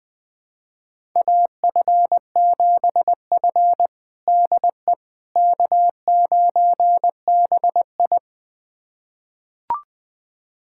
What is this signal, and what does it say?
Morse code, heard as a keyed tone, about 20 wpm, 700 Hz. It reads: AF7F DE K9BI
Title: USA Callsign DE Callsign - 4 Characters - 20wpm